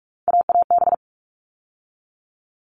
Morse code, heard as a keyed tone, about 44 wpm, 700 Hz. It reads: AAB
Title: All Cut Numbers - 3-Digit 45wpm